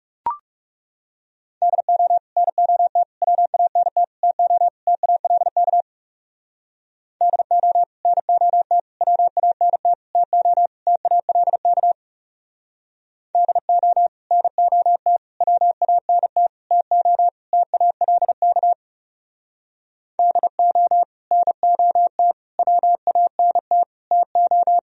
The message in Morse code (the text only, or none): DO NOT WANT TO TALK DO NOT WANT TO TALK DO NOT WANT TO TALK DO NOT WANT TO